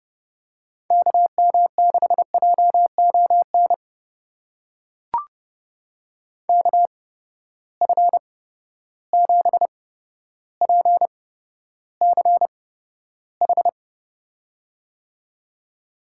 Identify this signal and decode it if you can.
Morse code, heard as a keyed tone, about 30 wpm, spaced as for 13 wpm, 700 Hz. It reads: KM6JON KF7PCH
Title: USA Call Signs - 6 Characters - 8x Character Spacing - 30wpm